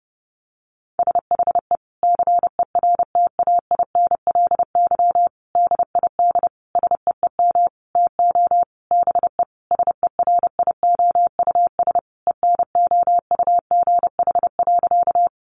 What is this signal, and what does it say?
Morse code, heard as a keyed tone, about 30 wpm, 700 Hz. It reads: SHE CERTAINLY DID SEEM TO BE SERIOUS ENOUGH.
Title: Sentences from Top 1100 Words 30wpm